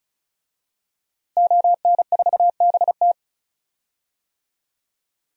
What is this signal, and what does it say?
Morse code, heard as a keyed tone, about 36 wpm, 700 Hz. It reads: ON4BT
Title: DXCC-100 Call Signs - 5 Characters - Repeated 2x - 35wpm